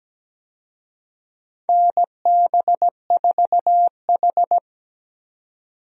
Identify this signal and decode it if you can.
Morse code, heard as a keyed tone, about 17 wpm, 700 Hz. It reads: NB4H
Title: USA Call Signs - 4 Characters - 17wpm